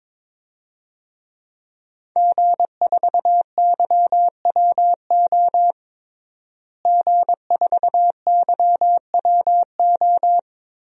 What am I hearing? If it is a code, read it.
Morse code, heard as a keyed tone, about 21 wpm, 700 Hz. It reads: G4YWO G4YWO
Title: DXCC-100 Call Signs - 5 Characters - Repeated 3x - 22wpm